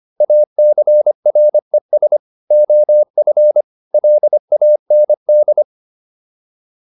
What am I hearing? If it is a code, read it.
Morse code, heard as a keyed tone, about 25 wpm, 600 Hz. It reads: ACRES OF LAND